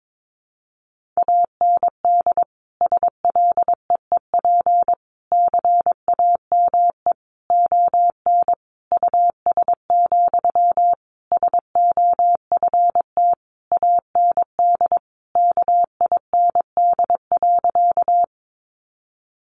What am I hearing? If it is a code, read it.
Morse code, heard as a keyed tone, about 22 wpm, 700 Hz. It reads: AND SLEEP CAME ON US, SOFT AND KIND.
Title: Sentences from Top 1500 Words 22wpm